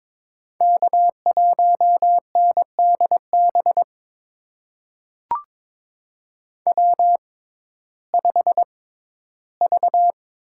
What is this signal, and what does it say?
Morse code, heard as a keyed tone, about 22 wpm, spaced as for 13 wpm, 700 Hz. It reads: K1NDB W5V
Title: USA Call Signs - 6x Character Spacing - 22wpm